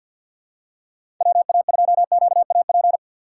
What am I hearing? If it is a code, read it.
Morse code, heard as a keyed tone, about 52 wpm, 700 Hz. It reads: WA2QAP
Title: USA Call Signs - 6 Characters - 4x Character Spacing - 50wpm